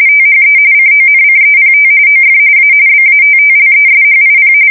RTTY, about 45.45 baud. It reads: THE CONCEPT OF A BATTLESHIP
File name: fsk170 / battleshipnj